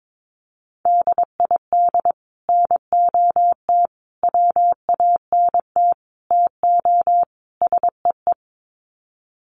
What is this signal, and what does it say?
Morse code, heard as a keyed tone, about 22 wpm, 700 Hz. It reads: DID NOT WANT TO SEE
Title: Sets of 5 Words Encore2 22wpm